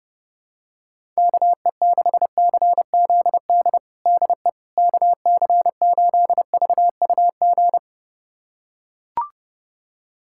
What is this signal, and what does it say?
Morse code, heard as a keyed tone, about 29 wpm, 700 Hz. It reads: KE6CZD DE KC8VUG
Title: USA Callsign DE Callsign - 6 Characters - 30wpm